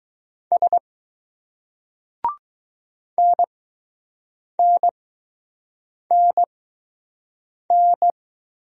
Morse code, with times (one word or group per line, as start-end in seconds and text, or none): 0.5-0.8 S
3.2-8.1 NNNN